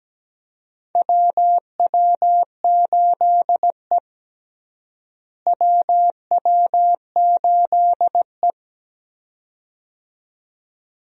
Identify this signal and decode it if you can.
Morse code, heard as a keyed tone, about 17 wpm, 700 Hz. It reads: WW8E WW8E
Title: USA Call Signs - 4 Characters - Repeated 2x - 17wpm